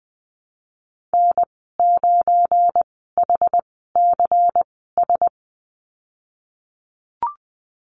Morse code, heard as a keyed tone, about 19 wpm, spaced as for 14 wpm, 700 Hz. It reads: N9HCS
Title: 173 Lesson 28g - Number 9 - Call Signs 20wpm